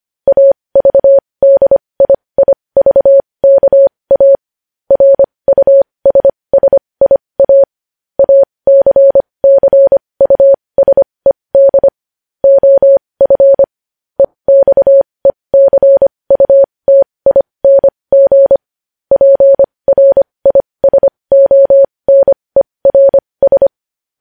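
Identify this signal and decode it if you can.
Morse code, heard as a keyed tone, about 25 wpm, spaced as for 20 wpm, 550 Hz. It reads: AVDIIVKA RUSSIA ACCUSED OF EXECUTING PRISONERS
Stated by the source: Morse practice of news headlines at 20 wpm.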